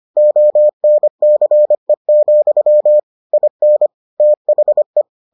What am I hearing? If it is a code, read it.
Morse code, heard as a keyed tone, about 25 wpm, 600 Hz. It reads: ONCE, IN THE